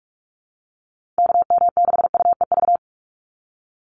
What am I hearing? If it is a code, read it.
Morse code, heard as a keyed tone, about 46 wpm, 700 Hz. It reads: KM6UEV
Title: USA Call Signs 45wpm